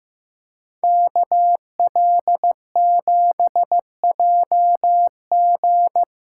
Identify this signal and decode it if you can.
Morse code, heard as a keyed tone, about 15 wpm, 700 Hz. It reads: KL7JG
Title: USA Callsign DE Callsign - 5 Characters - 15wpm